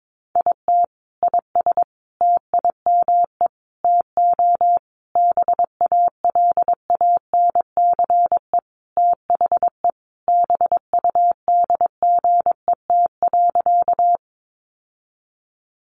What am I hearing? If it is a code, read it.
Morse code, heard as a keyed tone, about 22 wpm, 700 Hz. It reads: IT IS TIME TO BALANCE THE BUDGET.
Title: Sentences from Top 1200 Words 22wpm